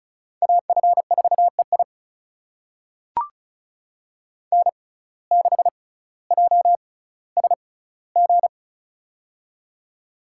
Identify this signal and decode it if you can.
Morse code, heard as a keyed tone, about 34 wpm, spaced as for 18 wpm, 700 Hz. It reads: AF4EI N6JSG
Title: USA Call Signs - 5 Characters - 6x Character Spacing - 35wpm